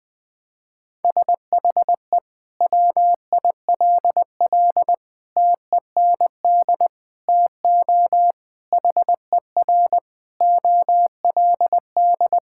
Morse code, heard as a keyed tone, about 20 wpm, 700 Hz. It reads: SHE WILL TEND TO HER OLD
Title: Sentences from Top 1000 Words - Review Part 5 - 20wpm